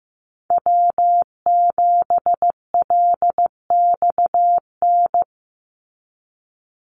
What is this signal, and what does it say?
Morse code, heard as a keyed tone, about 15 wpm, 700 Hz. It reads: W7LXN
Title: Call Signs Repeated 3x - 15wpm